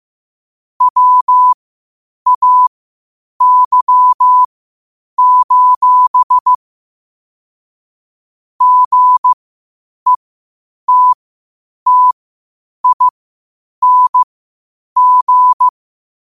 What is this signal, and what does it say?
Morse code, heard as a keyed tone, about 15 wpm, spaced as for 8 wpm, 1000 Hz. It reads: WAY: GETTING
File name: qotdfiles / 8wpm